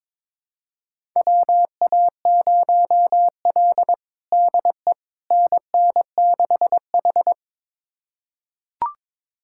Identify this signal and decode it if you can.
Morse code, heard as a keyed tone, about 22 wpm, 700 Hz. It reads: WA0L DE NN6H
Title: USA Callsign DE Callsign - 4 Characters - 22wpm